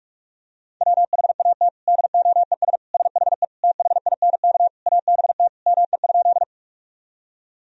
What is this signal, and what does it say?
Morse code, heard as a keyed tone, about 46 wpm, 700 Hz. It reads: WHAT DOES SHE THINK ABT ME?
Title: Sentences from Top 200 Words and Abbreviations 45wpm